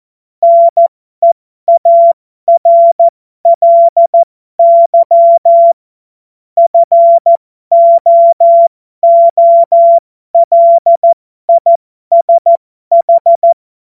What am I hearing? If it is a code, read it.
Morse code, heard as a keyed tone, about 14 wpm, spaced as for 12 wpm, 680 Hz. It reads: NEARLY FOOLISH
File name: Morsekurs / Witze